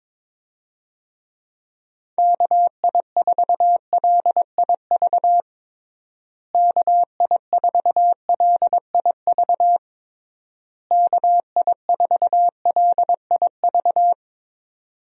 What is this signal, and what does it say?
Morse code, heard as a keyed tone, about 22 wpm, 700 Hz. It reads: KI4LIV KI4LIV KI4LIV
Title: USA Call Signs - 6 Characters - Repeated 3x - 22wpm